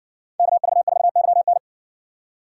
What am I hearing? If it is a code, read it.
Morse code, heard as a keyed tone, about 70 wpm, 700 Hz. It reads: KV4YN